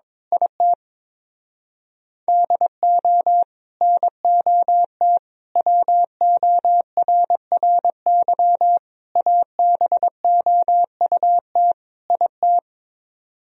Morse code, heard as a keyed tone, about 22 wpm, 700 Hz. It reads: IT DO NOT WORRY ABOUT IT
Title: Speed-Racing - Sets of 5 Words 22wpm